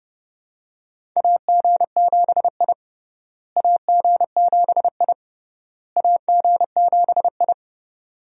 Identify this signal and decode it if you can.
Morse code, heard as a keyed tone, about 32 wpm, 700 Hz. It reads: AG7I AG7I AG7I
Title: USA Call Signs - 4 Characters - Repeated 3x - 30wpm